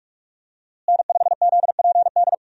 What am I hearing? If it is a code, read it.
Morse code, heard as a keyed tone, about 46 wpm, 700 Hz. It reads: N5ZPD
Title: USA Call Signs - 5 Characters - Repeated 2x - 45wpm